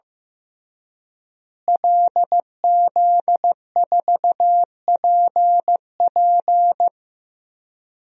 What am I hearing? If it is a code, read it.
Morse code, heard as a keyed tone, about 15 wpm, 700 Hz. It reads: LZ4PP
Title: DXCC-100 Call Signs - 5 Characters - Repeated 3x - 15wpm